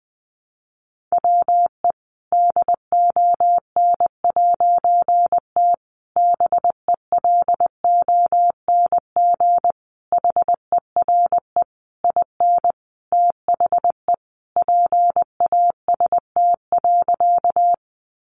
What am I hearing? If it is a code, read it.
Morse code, heard as a keyed tone, about 20 wpm, 700 Hz. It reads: WE DON'T BELONG HERE IN THE PAST.
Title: A Sound Of Thunder 20wpm